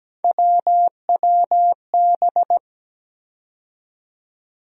Morse code, heard as a keyed tone, about 17 wpm, 700 Hz. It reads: WWB
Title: USA Call Sign Postfixes - 3 Letters 17wpm